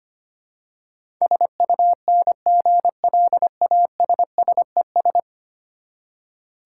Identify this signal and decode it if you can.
Morse code, heard as a keyed tone, about 25 wpm, 700 Hz. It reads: SUNGLASSES
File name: practice / 25wpm / Easy Words with Hints - 25wpm